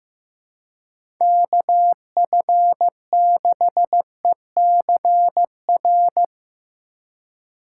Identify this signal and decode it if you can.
Morse code, heard as a keyed tone, about 16 wpm, 700 Hz. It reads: KF6ECR
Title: USA Call Signs - 6 Characters - 4x Character Spacing - 15wpm